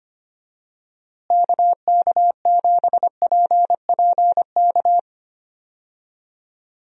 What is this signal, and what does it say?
Morse code, heard as a keyed tone, about 25 wpm, 700 Hz. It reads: KK7PPK
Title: USA Call Signs - 6 Characters - Repeated 2x - 25wpm